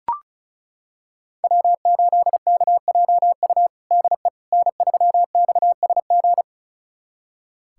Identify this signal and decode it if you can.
Morse code, heard as a keyed tone, about 36 wpm, 700 Hz. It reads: W8KJU DE N3XSG
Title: USA Callsign DE Callsign - 5 Characters - 35wpm